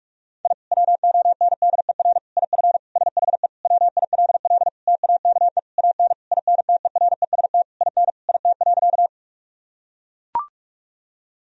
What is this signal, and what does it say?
Morse code, heard as a keyed tone, about 42 wpm, 700 Hz. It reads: I WONDER IF SHE WILL TAKE AN INTEREST IN IT.
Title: Sentences from Top 1000 Words - Review Part 3 - 45wpm